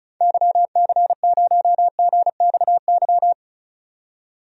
Morse code, heard as a keyed tone, about 34 wpm, 700 Hz. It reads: YC0GXY
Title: DXCC-100 Call Signs - 6 Characters - 35wpm